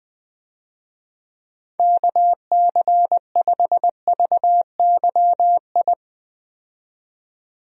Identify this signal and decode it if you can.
Morse code, heard as a keyed tone, about 20 wpm, 700 Hz. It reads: KC5VYI
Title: USA Call Signs - 6 Characters - 20wpm